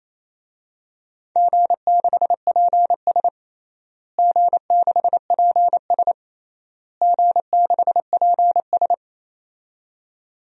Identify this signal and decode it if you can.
Morse code, heard as a keyed tone, about 28 wpm, 700 Hz. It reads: G6PS G6PS G6PS
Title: DXCC-100 Call Signs - 4 Characters - Repeated 3x - 28wpm